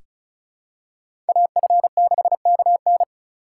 Morse code, heard as a keyed tone, about 36 wpm, 700 Hz. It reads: AF6KN
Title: USA Call Signs - 5 Characters - 4x Character Spacing - 35wpm